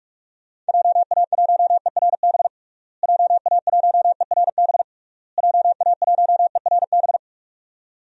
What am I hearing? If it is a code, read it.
Morse code, heard as a keyed tone, about 46 wpm, 700 Hz. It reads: JA1ERB JA1ERB JA1ERB
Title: DXCC-100 Call Signs - 6 Characters - Repeated 3x - 45wpm